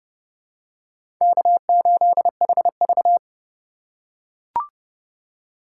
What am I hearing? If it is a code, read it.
Morse code, heard as a keyed tone, about 30 wpm, 700 Hz. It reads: K8HV
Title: USA Call Signs - 4 Characters - 3x Character Spacing - 30wpm